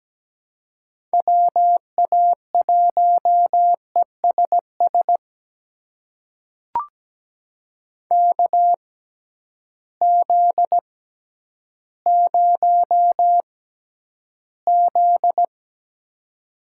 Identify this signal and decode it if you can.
Morse code, heard as a keyed tone, about 17 wpm, spaced as for 10 wpm, 700 Hz. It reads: WA1ESS KZ0Z
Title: USA Call Signs - 6x Character Spacing - 17wpm